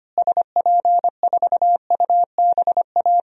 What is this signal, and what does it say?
Morse code, heard as a keyed tone, about 25 wpm, 700 Hz. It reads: SP4UBA
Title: DXCC-100 Call Signs - 6 Characters - Repeated 2x - 25wpm